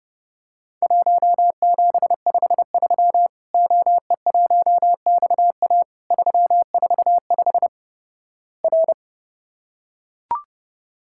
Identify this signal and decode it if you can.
Morse code, heard as a keyed tone, about 30 wpm, 700 Hz. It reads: 1753 OE1XA 345 R
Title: Worked All Europe DX Contest - QTC Exchange - 30wpm